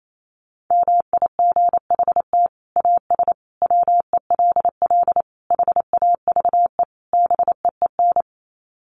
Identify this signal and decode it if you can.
Morse code, heard as a keyed tone, about 28 wpm, 700 Hz. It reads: MIGHT AS WELL HAVE BEEN
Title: Sets of 5 Words Encore2 28wpm